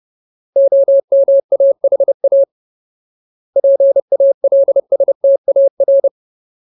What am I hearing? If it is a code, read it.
Morse code, heard as a keyed tone, about 30 wpm, 550 Hz. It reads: OMAHA PALSTAR
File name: Ultimate Word List / 30WPM Ultimate